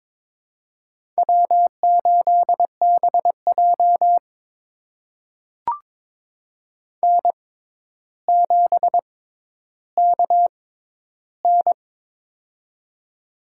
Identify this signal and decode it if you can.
Morse code, heard as a keyed tone, about 23 wpm, spaced as for 11 wpm, 700 Hz. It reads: W8BJ N7KN
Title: USA Call Signs - 4 Characters - 6x Character Spacing - 22wpm